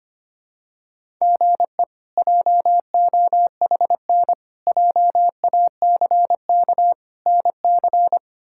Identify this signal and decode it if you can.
Morse code, heard as a keyed tone, about 25 wpm, 700 Hz. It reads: GE JOHN JACK NC
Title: K1USN SST QSO Practice - 25wpm